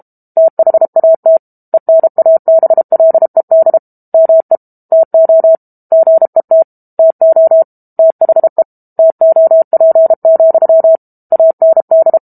Morse code, heard as a keyed tone, about 32 wpm, 650 Hz. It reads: THAT ENABLED ME TO GET TO THE TOP, AND
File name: Book Two / 06-Chapter 6 - 32 WPM